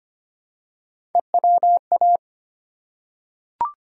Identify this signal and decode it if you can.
Morse code, heard as a keyed tone, about 25 wpm, 700 Hz. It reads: EWA
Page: morse-code-ninja-practice-25wpm